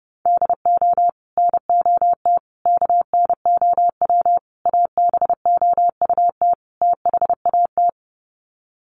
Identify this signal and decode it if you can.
Morse code, heard as a keyed tone, about 30 wpm, 700 Hz. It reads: DO NOT KNOW ABOUT THAT
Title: Sets of 5 Words Encore2 30wpm